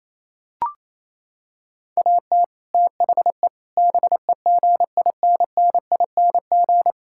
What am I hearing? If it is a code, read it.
Morse code, heard as a keyed tone, about 27 wpm, 700 Hz. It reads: AT THE BEGINNING
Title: Sentences from Top 1000 Words - Review Part 5 - 28wpm